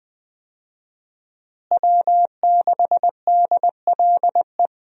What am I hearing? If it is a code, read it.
Morse code, heard as a keyed tone, about 20 wpm, 700 Hz. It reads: W6DLE
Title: USA Call Signs - 5 Characters - Repeated 3x - 20wpm